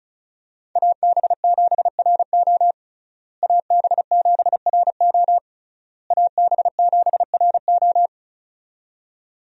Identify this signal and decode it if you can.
Morse code, heard as a keyed tone, about 36 wpm, 700 Hz. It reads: AB7RO AB7RO AB7RO
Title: USA Call Signs - 5 Characters - Repeated 3x - 35wpm